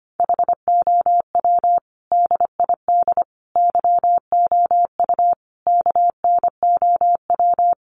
Morse code, heard as a keyed tone, about 25 wpm, 700 Hz. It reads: HOW DID YOU KNOW
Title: Sentences from Top 100 Words 25wpm